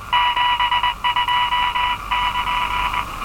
Morse code, heard as a keyed tone, about 19 wpm, 970 Hz. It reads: 72 <KN>
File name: attachments / RV3GM EW1CY